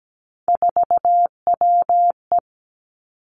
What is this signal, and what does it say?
Morse code, heard as a keyed tone, about 17 wpm, 700 Hz. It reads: 4WE
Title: Three Challenging Letter-Number Combinations 17wpm